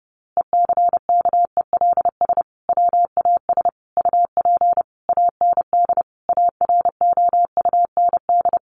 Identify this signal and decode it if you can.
Morse code, heard as a keyed tone, about 30 wpm, 700 Hz. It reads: ECKELS WAS UP AND AROUND
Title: A Sound Of Thunder 30wpm